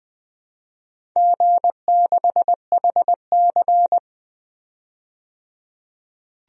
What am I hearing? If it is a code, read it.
Morse code, heard as a keyed tone, about 20 wpm, 700 Hz. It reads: G6HC
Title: DXCC-100 Call Signs - 4 Characters - 20wpm